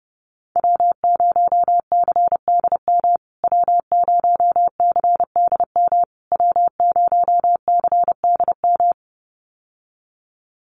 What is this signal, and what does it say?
Morse code, heard as a keyed tone, about 30 wpm, 700 Hz. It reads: W0CDM W0CDM W0CDM
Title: USA Call Signs Repeated 3x 30wpm